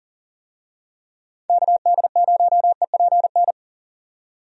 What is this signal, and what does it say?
Morse code, heard as a keyed tone, about 40 wpm, 700 Hz. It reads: KD0EPN
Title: USA Call Signs - 6 Characters - 40wpm